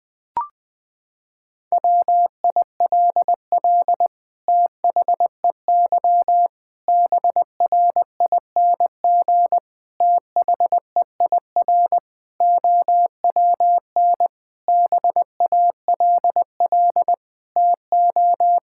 Morse code, heard as a keyed tone, about 20 wpm, 700 Hz. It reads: WILL THEY BRING THEIR OWN BALL TO
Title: Sentences from Top 1000 Words - Review Part 6 - 20wpm